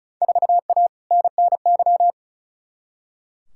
Morse code, heard as a keyed tone, about 36 wpm, 700 Hz. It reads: 4A NNY
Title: Field Day Exchanges Encore - 35wpm